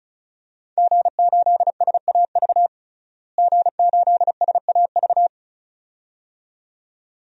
Morse code, heard as a keyed tone, about 34 wpm, 700 Hz. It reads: G8SAV G8SAV
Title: DXCC-100 Call Signs - 5 Characters - Repeated 2x - 35wpm